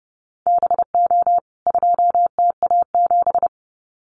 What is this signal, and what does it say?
Morse code, heard as a keyed tone, about 30 wpm, 700 Hz. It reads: BO 2TA7
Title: CWT Exchange 30wpm